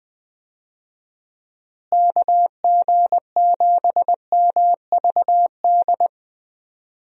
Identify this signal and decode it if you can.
Morse code, heard as a keyed tone, about 20 wpm, 700 Hz. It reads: KG7MVD
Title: USA Call Signs - 6 Characters - Repeated 2x - 20wpm